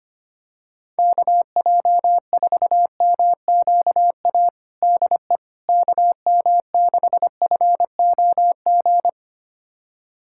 Morse code, heard as a keyed tone, about 25 wpm, 700 Hz. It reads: KJ4MQA DE KM6FOG
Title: USA Callsign DE Callsign - 6 Characters - 25wpm